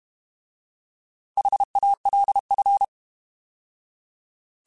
Morse code, heard as a keyed tone, about 32 wpm, 800 Hz. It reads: HALF